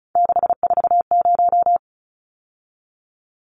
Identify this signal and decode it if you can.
Morse code, heard as a keyed tone, about 34 wpm, 700 Hz. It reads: 640
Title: Numbers - 3-Digit 35wpm